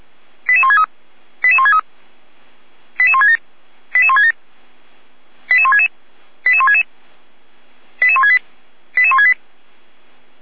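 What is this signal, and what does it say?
Five-tone selective call (ZVEI), heard as quick runs of five tones, tones of about 80 ms. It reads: 88163 88163 88167 88167 88166 88166 88168 88168
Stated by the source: Five-tone ZVEI selective call received with an SDR.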